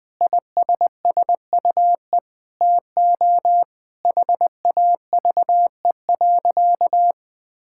Morse code, heard as a keyed tone, about 20 wpm, 700 Hz. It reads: ISSUE TO HAVE.
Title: Sentences from Top 1000 Words - Review Part 6 - 20wpm